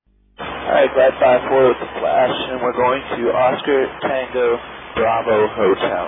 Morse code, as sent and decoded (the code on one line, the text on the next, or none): none